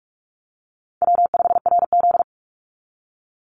Morse code, heard as a keyed tone, about 44 wpm, 700 Hz. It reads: W5RZ
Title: Easy Call Signs Repeated 3x - 45wpm